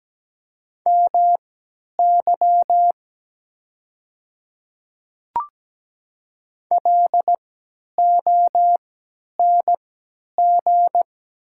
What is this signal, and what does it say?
Morse code, heard as a keyed tone, about 17 wpm, spaced as for 10 wpm, 700 Hz. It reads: MY LONG
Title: 3x Character Spacing - Top 200 Words - 17wpm